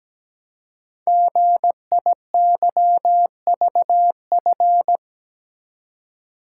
Morse code, heard as a keyed tone, about 17 wpm, 700 Hz. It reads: GIYVF
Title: Code Groups - 5 Letters - 17wpm